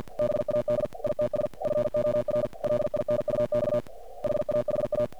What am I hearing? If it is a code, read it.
Morse code, heard as a keyed tone, about 50 wpm, 600 Hz, with noise. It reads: BAD ITS FOR LINUX HAHA
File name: archived / CRSnet